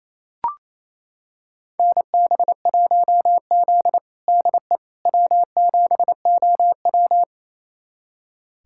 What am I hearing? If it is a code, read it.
Morse code, heard as a keyed tone, about 28 wpm, 700 Hz. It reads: NB1Z DE W7OW
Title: USA Callsign DE Callsign - 4 Characters - 28wpm